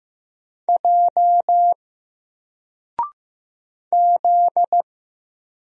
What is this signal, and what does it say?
Morse code, heard as a keyed tone, about 15 wpm, 700 Hz. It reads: J Z
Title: Single Letter-Number Alt 15wpm